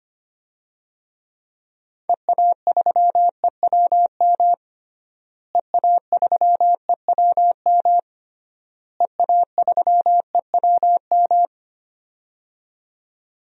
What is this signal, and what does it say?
Morse code, heard as a keyed tone, about 25 wpm, 700 Hz. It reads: EA3EWM EA3EWM EA3EWM
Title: DXCC-100 Call Signs - 6 Characters - Repeated 3x - 25wpm